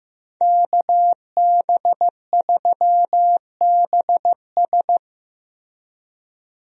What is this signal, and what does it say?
Morse code, heard as a keyed tone, about 15 wpm, 700 Hz. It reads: KB3BS
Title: USA Call Signs - 5 Characters - Repeated 3x - 15wpm